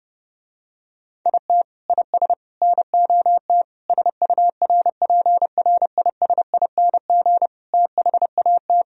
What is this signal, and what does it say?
Morse code, heard as a keyed tone, about 30 wpm, 700 Hz. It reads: IT IS NOT SURPRISING THAT